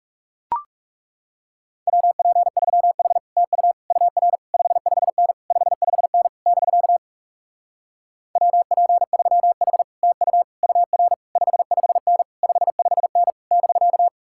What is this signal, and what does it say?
Morse code, heard as a keyed tone, about 40 wpm, 700 Hz. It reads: WP3H TU UR 55N 55N <BK> WP3H TU UR 55N 55N <BK>